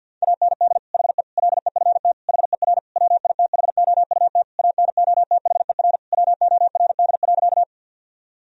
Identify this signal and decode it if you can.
Morse code, heard as a keyed tone, about 50 wpm, 700 Hz. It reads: AND HE LEFT HER WITHOUT ANOTHER WORD.
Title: Sentences from Top 1000 Words - Review Part 3 - 50wpm